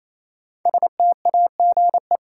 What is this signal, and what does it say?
Morse code, heard as a keyed tone, about 28 wpm, 700 Hz. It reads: STAGE
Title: Sentences from Top 1200 Words - New Word Drill Alt - 28wpm